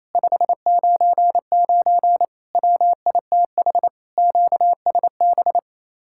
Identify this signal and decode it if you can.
Morse code, heard as a keyed tone, about 34 wpm, 700 Hz. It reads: 599 WITH QSB
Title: Speed-Racing Common QSO Phrases Encore - 28wpm